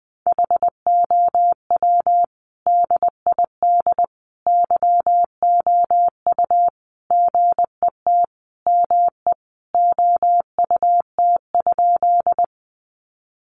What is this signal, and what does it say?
Morse code, heard as a keyed tone, about 20 wpm, 700 Hz. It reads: HOW DID YOU GET ME OUT?
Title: Sentences from Top 200 Words 20wpm